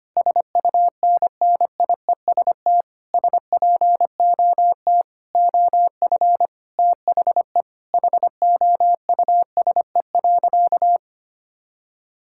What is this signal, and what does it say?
Morse code, heard as a keyed tone, about 25 wpm, 700 Hz. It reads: SUNNIEST SPOT OF THE HOUSE.
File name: practice / 25wpm / Difficult Sentences with Hints - 25wpm